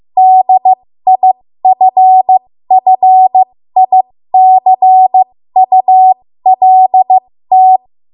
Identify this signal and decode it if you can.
Morse code, heard as a keyed tone, about 15 wpm, spaced as for 13 wpm, 750 Hz. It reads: DIFFICULT
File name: Archive / 13 WPM / 13WPM